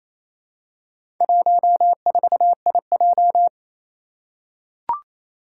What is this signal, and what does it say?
Morse code, heard as a keyed tone, about 27 wpm, 700 Hz. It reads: 14IJ
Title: Four Challenging Letter-Number Combinations 28wpm